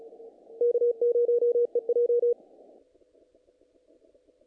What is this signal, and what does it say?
Morse code, heard as a keyed tone, about 36 wpm, 480 Hz. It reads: K0EJ
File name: contests / naqp-cw-k5tr-op-k5pi- -1800